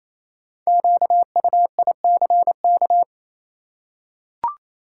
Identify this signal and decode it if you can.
Morse code, heard as a keyed tone, about 28 wpm, 700 Hz. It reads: QUICK
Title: Five Letter Words - 28wpm